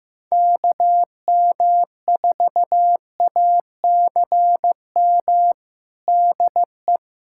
Morse code, heard as a keyed tone, about 15 wpm, 700 Hz. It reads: KM4ACM DE
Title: USA Callsign DE Callsign 15wpm